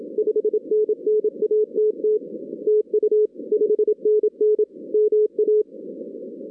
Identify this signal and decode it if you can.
Morse code, heard as a keyed tone, about 27 wpm, 440 Hz. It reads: 5NN ATT TU 5NN MA
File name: 2016 arrl dx cw / 2016-ARRL-DX-CW-K5TR- - 0-41